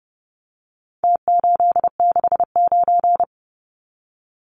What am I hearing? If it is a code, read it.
Morse code, heard as a keyed tone, about 30 wpm, 700 Hz. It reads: T869
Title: Numbers - Contest Style - 4-Digit 30wpm